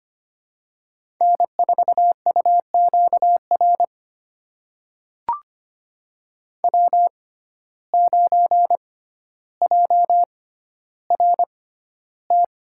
Morse code, heard as a keyed tone, about 25 wpm, spaced as for 13 wpm, 700 Hz. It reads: N4UQR W9JRT
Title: USA Call Signs - 5 Characters - 6x Character Spacing - 25wpm